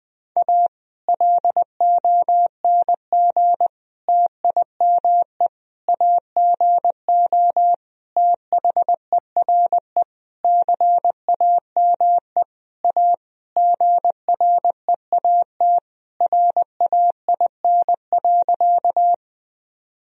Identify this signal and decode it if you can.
Morse code, heard as a keyed tone, about 20 wpm, 700 Hz. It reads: A LONG TIME AGO THERE CAME A GREAT RAIN.
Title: Sentences from Top 1000 Words - Review Part 3 - 20wpm